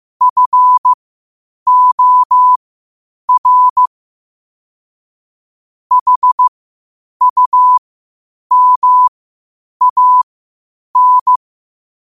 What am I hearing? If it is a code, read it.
Morse code, heard as a keyed tone, about 15 wpm, spaced as for 8 wpm, 1000 Hz. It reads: FOR HUMAN
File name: qotdfiles / 8wpm